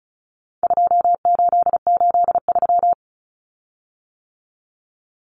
Morse code, heard as a keyed tone, about 34 wpm, 700 Hz. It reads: 2883
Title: Common Cut Numbers - 4-Digit 35wpm